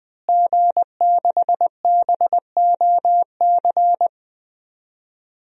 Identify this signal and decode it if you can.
Morse code, heard as a keyed tone, about 20 wpm, 700 Hz. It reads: G6BOC
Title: DXCC-100 Call Signs - 5 Characters - Repeated 3x - 20wpm